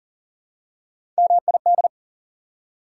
Morse code, heard as a keyed tone, about 40 wpm, 700 Hz. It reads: MID
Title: Prefixes and Suffixes 40wpm